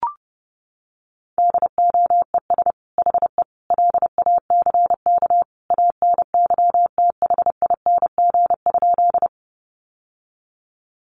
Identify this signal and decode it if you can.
Morse code, heard as a keyed tone, about 30 wpm, 700 Hz. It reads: DOES HE LACK ANYTHING?
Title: Sentences from Top 1100 Words 30wpm